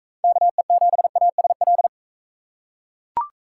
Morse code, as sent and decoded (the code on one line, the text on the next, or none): -.- . --... .- ... .-..
KE7ASL